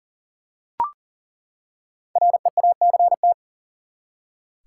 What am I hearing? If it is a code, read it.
Morse code, heard as a keyed tone, about 40 wpm, 700 Hz. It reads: REACT
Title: Five Letter Words - 40wpm